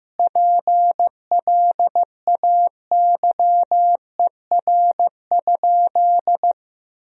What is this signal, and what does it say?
Morse code, heard as a keyed tone, about 16 wpm, 700 Hz. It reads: PLAYER?
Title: Sentences from Top 1000 Words - Review Part 4 - 15wpm